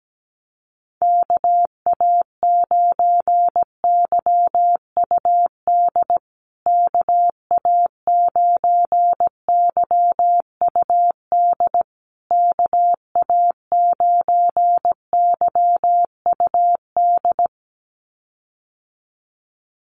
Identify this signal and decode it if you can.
Morse code, heard as a keyed tone, about 17 wpm, 700 Hz. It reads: KA9YUD KA9YUD KA9YUD
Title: USA Call Signs Repeated 3x 17wpm